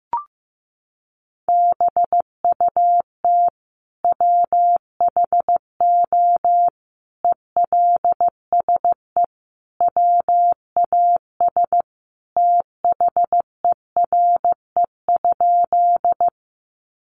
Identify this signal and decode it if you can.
Morse code, heard as a keyed tone, about 15 wpm, 700 Hz. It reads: BUT WHO ELSE WAS THERE?
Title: Sentences from Top 800 Words 15wpm